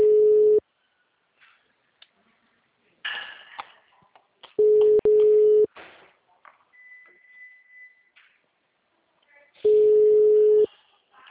Ringback tone (Europe, 425 Hz).